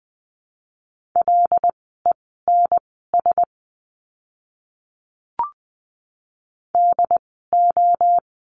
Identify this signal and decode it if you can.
Morse code, heard as a keyed tone, about 20 wpm, spaced as for 14 wpm, 700 Hz. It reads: LENS DO
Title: 075 Lesson 13f - Letter L - Words 20wpm